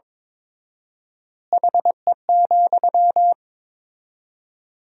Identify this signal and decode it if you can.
Morse code, heard as a keyed tone, about 22 wpm, 700 Hz. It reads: HE,
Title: 4x Word-Spacing - Sentences from Top 500 Words - 22wpm